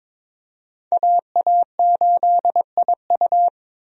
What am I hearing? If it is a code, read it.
Morse code, heard as a keyed tone, about 22 wpm, 700 Hz. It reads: AA8IU
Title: USA Call Signs - 5 Characters - 22wpm